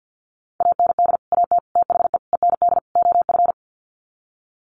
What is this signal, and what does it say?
Morse code, heard as a keyed tone, about 50 wpm, 700 Hz. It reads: AND AT THE END OF
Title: Sets of 5 Words 50wpm